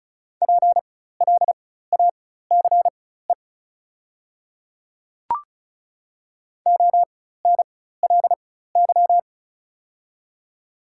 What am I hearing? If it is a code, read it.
Morse code, heard as a keyed tone, about 36 wpm, spaced as for 16 wpm, 700 Hz. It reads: PLACE ONLY
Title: 4x Character Spacing - Top 200 Words - 35wpm